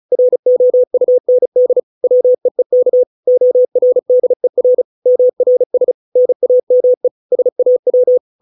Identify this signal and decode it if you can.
Morse code, heard as a keyed tone, about 34 wpm, 500 Hz. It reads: ROUND WEEK ORDER MRS NAME SAW